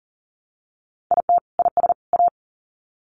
Morse code, heard as a keyed tone, about 40 wpm, 700 Hz. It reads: IT IS A
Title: Sets of 3 Words 40wpm